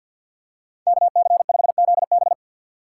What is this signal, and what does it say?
Morse code, heard as a keyed tone, about 49 wpm, 700 Hz. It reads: KC5ZB
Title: USA Call Signs - 3x Character Spacing - 50wpm